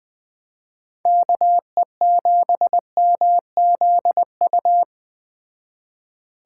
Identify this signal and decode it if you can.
Morse code, heard as a keyed tone, about 20 wpm, 700 Hz. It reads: KE7MZU